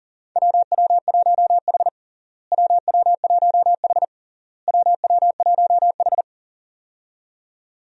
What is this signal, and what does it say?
Morse code, heard as a keyed tone, about 37 wpm, 700 Hz. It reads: WW1H WW1H WW1H
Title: USA Call Signs - 4 Characters - Repeated 3x - 40wpm